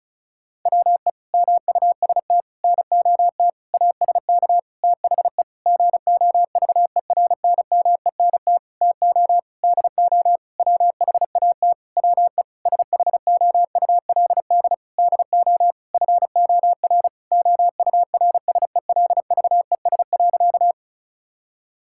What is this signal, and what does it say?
Morse code, heard as a keyed tone, about 34 wpm, 700 Hz. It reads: WE MUST NOT ASK THE GOVERNMENT TO DO WHAT WE SHOULD DO FOR OURSELVES.
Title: Sentences from Top 1600 Words 35wpm